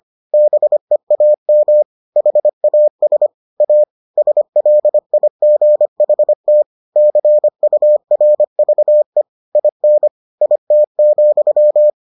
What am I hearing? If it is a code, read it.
Morse code, heard as a keyed tone, about 25 wpm, 600 Hz. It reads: BEAM HAS A SLIGHT CURVE IN IT,